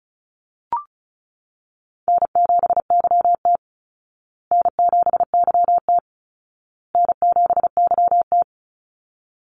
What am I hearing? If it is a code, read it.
Morse code, heard as a keyed tone, about 34 wpm, 700 Hz. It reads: N7YT N7YT N7YT
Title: Call Signs Repeated 2x - 35wpm